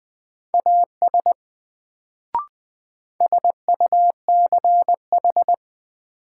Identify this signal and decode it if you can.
Morse code, heard as a keyed tone, about 20 wpm, 700 Hz. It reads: AS SUCH
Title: Sets of 2 Words - New Word Drill - 20wpm